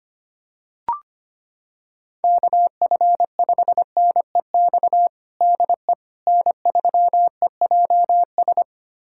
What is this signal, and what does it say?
Morse code, heard as a keyed tone, about 25 wpm, 700 Hz. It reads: KF5NEX DE N3EJS
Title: USA Callsign DE Callsign 25wpm